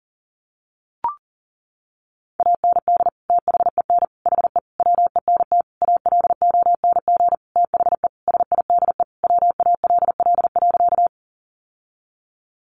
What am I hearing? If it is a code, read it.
Morse code, heard as a keyed tone, about 40 wpm, 700 Hz. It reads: AND THEN HE WENT ALONG THE SIDE WALL.
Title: Sentences from Top 1100 Words 40wpm